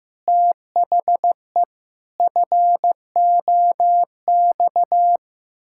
Morse code, heard as a keyed tone, about 15 wpm, 700 Hz. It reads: THE FOX
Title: Pangram Challenge v2 - 15wpm